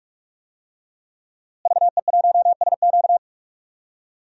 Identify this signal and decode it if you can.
Morse code, heard as a keyed tone, about 46 wpm, 700 Hz. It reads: VE1SQ
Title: DXCC-100 Call Signs - 5 Characters - Repeated 2x - 45wpm